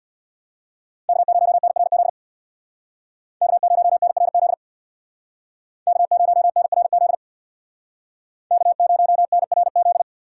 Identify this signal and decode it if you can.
Morse code, heard as a keyed tone, about 64 wpm, 700 Hz. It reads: K0NRZ K0NRZ K0NRZ K0NRZ